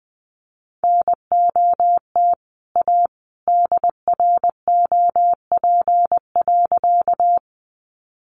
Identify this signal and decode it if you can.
Morse code, heard as a keyed tone, about 19 wpm, 700 Hz. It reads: NOT A DROP.
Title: Sentences from Top 1400 Words 20wpm